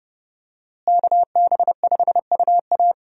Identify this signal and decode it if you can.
Morse code, heard as a keyed tone, about 29 wpm, 700 Hz. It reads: KB5UA
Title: USA Call Signs - 5 Characters - Repeated 3x - 30wpm